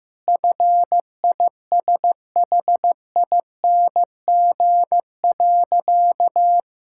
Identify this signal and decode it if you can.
Morse code, heard as a keyed tone, about 16 wpm, 700 Hz. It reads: FISHING.
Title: Sentences from Top 1000 Words - Review Part 5 - 15wpm